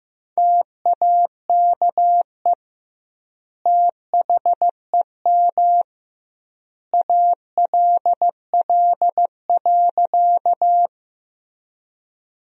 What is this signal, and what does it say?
Morse code, heard as a keyed tone, about 15 wpm, 700 Hz. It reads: TAKE THEM ALL.
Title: 2x Word-Spacing - Sentences from Top 200 Words - 15wpm